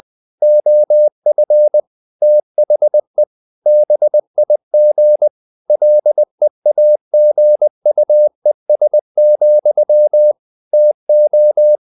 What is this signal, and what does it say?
Morse code, heard as a keyed tone, about 20 wpm, 600 Hz. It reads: OF THE BIG LEAGUES, TO
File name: the old man and the sea / the old man and the sea part 1 20wpm